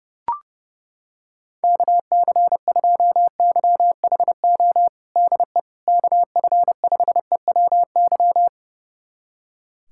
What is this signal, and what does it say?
Morse code, heard as a keyed tone, about 32 wpm, 700 Hz. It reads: KC2YHO DE KF5EWY
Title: USA Callsign DE Callsign - 6 Characters - 30wpm